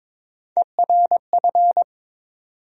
Morse code, heard as a keyed tone, about 22 wpm, 700 Hz. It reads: ERF
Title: USA Call Sign Postfixes - 3 Letters 22wpm